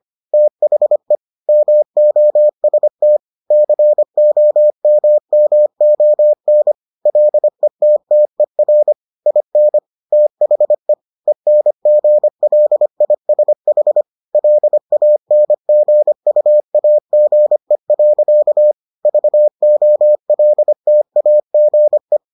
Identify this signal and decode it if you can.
Morse code, heard as a keyed tone, about 25 wpm, 600 Hz. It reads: THE MOST COMMON LETTER IN THE ENGLISH LANGUAGE. VOLTAGE